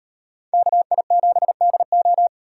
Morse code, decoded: KI7DO

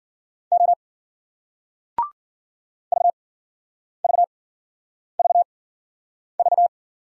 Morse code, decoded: O VVVV

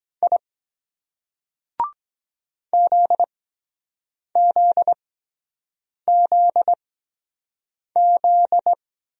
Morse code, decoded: I ZZZZ

